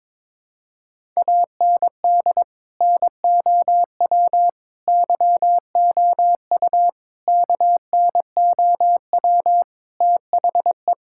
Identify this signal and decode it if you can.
Morse code, heard as a keyed tone, about 21 wpm, 700 Hz. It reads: AND NOW YOU KNOW THE
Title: Sentences from Top 1000 Words - Review Part 6 - 22wpm